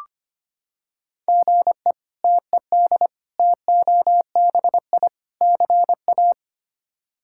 Morse code, heard as a keyed tone, about 25 wpm, 700 Hz. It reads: GE TED TOBI CA
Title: K1USN SST QSO Practice - 25wpm